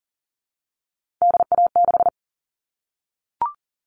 Morse code, heard as a keyed tone, about 40 wpm, 700 Hz. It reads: DA6
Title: All Cut Numbers - 3-Digit 40wpm